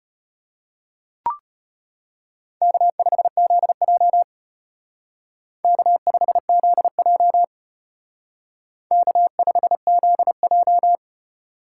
Morse code, heard as a keyed tone, about 32 wpm, 700 Hz. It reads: K5ZJ K5ZJ K5ZJ